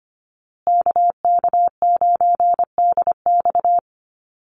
Morse code, heard as a keyed tone, about 25 wpm, 700 Hz. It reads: KK9DX
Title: Call Signs Repeated 2x - 25wpm